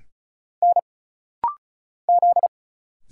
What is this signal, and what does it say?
Morse code, heard as a keyed tone, about 34 wpm, 700 Hz. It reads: N Z
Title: Single Letter-Number Alt - Rapid-Fire 35wpm